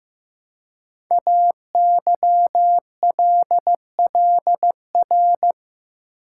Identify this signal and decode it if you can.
Morse code, heard as a keyed tone, about 15 wpm, 700 Hz. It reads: AYLLR
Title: Code Groups - 5 Letters - 15wpm